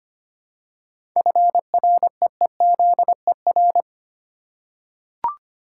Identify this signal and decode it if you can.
Morse code, heard as a keyed tone, about 25 wpm, 700 Hz. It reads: FREEZER
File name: practice / 25wpm / Easy Words with Hints - 25wpm